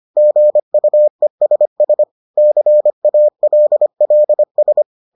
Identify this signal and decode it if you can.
Morse code, heard as a keyed tone, about 25 wpm, 600 Hz. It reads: GUESS CALLS